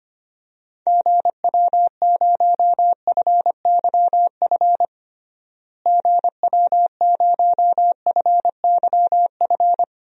Morse code, heard as a keyed tone, about 25 wpm, 700 Hz. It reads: GW0FYF GW0FYF
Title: DXCC-100 Call Signs - 6 Characters - Repeated 3x - 25wpm